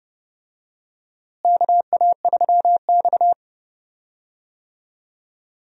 Morse code, heard as a keyed tone, about 30 wpm, 700 Hz. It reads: KA3X
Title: USA Call Signs - 4 Characters - Repeated 3x - 30wpm